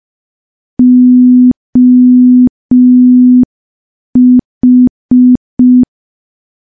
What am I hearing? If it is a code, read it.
Morse code, heard as a keyed tone, about 5 wpm, 250 Hz. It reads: OH
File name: morse / morse-training-05WPM